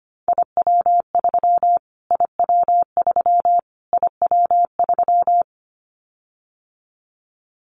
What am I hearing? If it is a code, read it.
Morse code, heard as a keyed tone, about 25 wpm, 700 Hz. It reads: IW3 IW3 IW3
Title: Callsign Prefixes for Top 25 DXCC - Repeated 3x - 25wpm